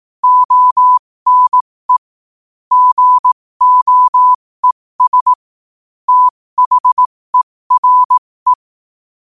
Morse code, heard as a keyed tone, about 18 wpm, spaced as for 15 wpm, 1000 Hz. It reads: ONE GOES THERE
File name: qotdfiles / 15wpm